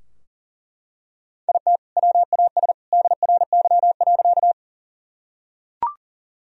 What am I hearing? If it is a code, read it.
Morse code, heard as a keyed tone, about 38 wpm, 700 Hz. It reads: IT WAS DRY.
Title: Sentences from Top 1000 Words - Review Part 3 - 40wpm